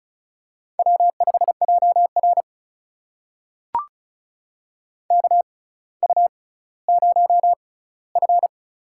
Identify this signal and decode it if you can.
Morse code, heard as a keyed tone, about 36 wpm, spaced as for 18 wpm, 700 Hz. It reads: W5JR KU0F
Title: USA Call Signs - 4 Characters - 6x Character Spacing - 35wpm